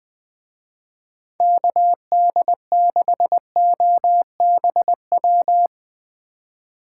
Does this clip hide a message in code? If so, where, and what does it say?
Yes: Morse code from 1.4 to 5.7 s, heard as a keyed tone, about 20 wpm, 700 Hz. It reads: KD6OBW